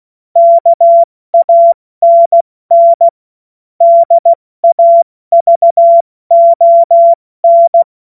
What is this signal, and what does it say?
Morse code, heard as a keyed tone, about 16 wpm, 680 Hz. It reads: KANN DAVON
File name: Morsekurs / Witze / Witz 21. Juni 2022 70 F80